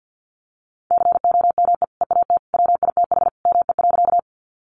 Morse code, heard as a keyed tone, about 50 wpm, 700 Hz. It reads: COME EAT WITH ME.
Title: Sentences from Top 300 Words 50wpm